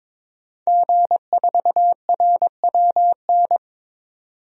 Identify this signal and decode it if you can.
Morse code, heard as a keyed tone, about 22 wpm, 700 Hz. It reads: G4RWN
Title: DXCC-100 Call Signs - 5 Characters - 22wpm